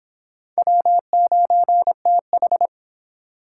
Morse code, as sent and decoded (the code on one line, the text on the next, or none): .-- ----. - ....
W9TH